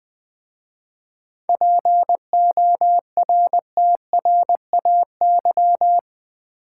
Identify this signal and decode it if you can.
Morse code, heard as a keyed tone, about 20 wpm, 700 Hz. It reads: PORTRAY
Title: 7 Letter Words - First 3 Letters Given - 20wpm